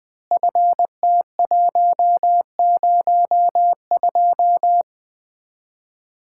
Morse code, heard as a keyed tone, about 20 wpm, 700 Hz. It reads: FT102
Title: Common QSO Phrases - 4x Word-Spacing - 20wpm